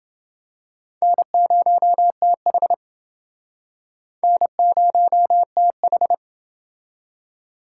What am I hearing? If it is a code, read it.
Morse code, heard as a keyed tone, about 28 wpm, 700 Hz. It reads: N0TH N0TH